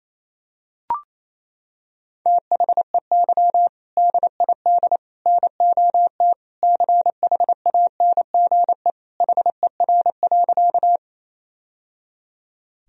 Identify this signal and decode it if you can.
Morse code, heard as a keyed tone, about 28 wpm, 700 Hz. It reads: THEY DID NOT CHANGE HER.
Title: Sentences from Top 1000 Words - Review Part 1 - 28wpm